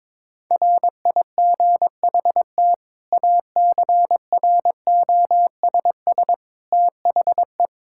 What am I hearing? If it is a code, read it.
Morse code, heard as a keyed tone, about 22 wpm, 700 Hz. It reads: RIGHT ACROSS THE
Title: Sentences from Top 1000 Words - Review Part 5 - 22wpm